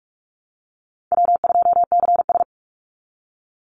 Morse code, heard as a keyed tone, about 44 wpm, 700 Hz. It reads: W2CS
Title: Easy Call Signs Repeated 2x - 45wpm